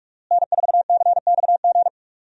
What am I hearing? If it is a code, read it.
Morse code, heard as a keyed tone, about 46 wpm, 700 Hz. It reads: N4CXG
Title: USA Call Signs - 5 Characters - Repeated 3x - 45wpm